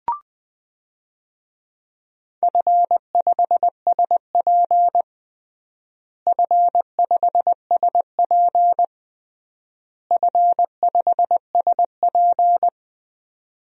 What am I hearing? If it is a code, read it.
Morse code, heard as a keyed tone, about 21 wpm, 700 Hz. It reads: F5SP F5SP F5SP
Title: DXCC-100 Call Signs - 4 Characters - Repeated 3x - 20wpm